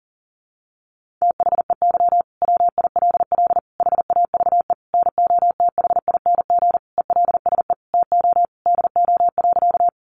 Morse code, heard as a keyed tone, about 40 wpm, 700 Hz. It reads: THEY WILL HAVE NOTHING ELSE TO DO.
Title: Sentences from Top 800 Words 40wpm